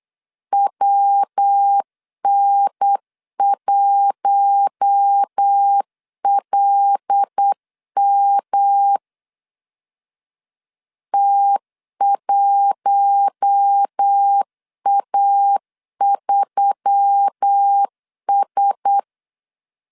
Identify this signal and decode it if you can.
Morse code, heard as a keyed tone, about 8 wpm, 800 Hz. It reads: WN1LM T1A3S
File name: CW / DET / 1 6 1500C S B 8wpm 5142